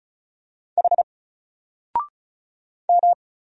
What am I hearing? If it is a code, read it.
Morse code, heard as a keyed tone, about 36 wpm, 700 Hz. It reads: H M
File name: Single-Letter-Number-35wpm